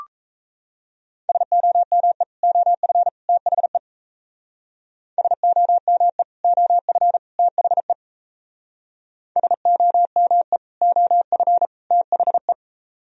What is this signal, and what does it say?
Morse code, heard as a keyed tone, about 37 wpm, 700 Hz. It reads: SOME OF THE SOME OF THE SOME OF THE